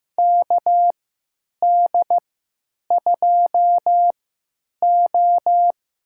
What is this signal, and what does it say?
Morse code, heard as a keyed tone, about 15 wpm, spaced as for 9 wpm, 700 Hz. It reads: KD2O
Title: USA Call Signs - 3x Character Spacing - 15wpm